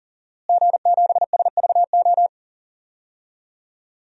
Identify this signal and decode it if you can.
Morse code, heard as a keyed tone, about 40 wpm, 700 Hz. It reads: G7SVO
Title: DXCC-100 Call Signs - 5 Characters - Repeated 3x - 40wpm